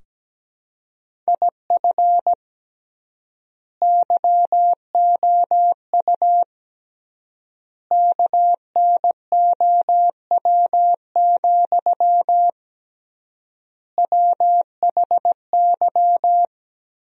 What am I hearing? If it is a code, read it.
Morse code, heard as a keyed tone, about 17 wpm, 700 Hz. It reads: IF YOU KNOW, WHY
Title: 3x Word-Spacing - Sentences from Top 300 Words - 17wpm